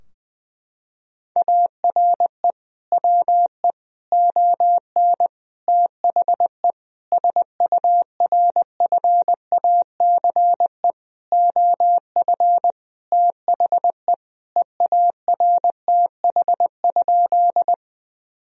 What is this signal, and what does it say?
Morse code, heard as a keyed tone, about 20 wpm, 700 Hz. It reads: ARE WE ON THE SURFACE OF THE EARTH?
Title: Sentences from Top 1000 Words - Review Part 3 - 20wpm